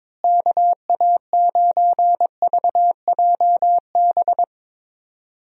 Morse code, heard as a keyed tone, about 22 wpm, 700 Hz. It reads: KA9VJB
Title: USA Call Signs - 6 Characters - Repeated 2x - 22wpm